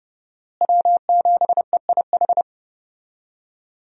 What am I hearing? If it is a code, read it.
Morse code, heard as a keyed tone, about 30 wpm, 700 Hz. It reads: W7EIH
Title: USA Call Signs - 5 Characters - Repeated 2x - 30wpm